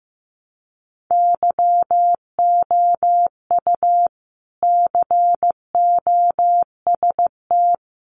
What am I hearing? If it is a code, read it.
Morse code, heard as a keyed tone, about 15 wpm, 700 Hz. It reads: YOU COST
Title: Sentences from Top 800 Words 15wpm